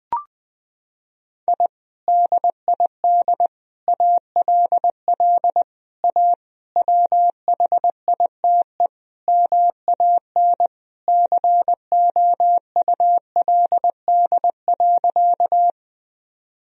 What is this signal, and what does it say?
Morse code, heard as a keyed tone, about 20 wpm, 700 Hz. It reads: I DID ALL A WHITE MAN COULD.
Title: Sentences from Top 1000 Words - Review Part 1 - 20wpm